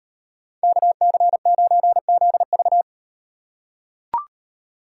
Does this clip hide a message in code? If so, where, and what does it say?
Yes: Morse code from 0.6 to 2.8 s, heard as a keyed tone, about 36 wpm, 700 Hz. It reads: KC9ZV